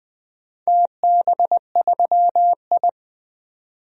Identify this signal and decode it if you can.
Morse code, heard as a keyed tone, about 20 wpm, 700 Hz. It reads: TB3I
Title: DXCC-100 Call Signs - 4 Characters - 20wpm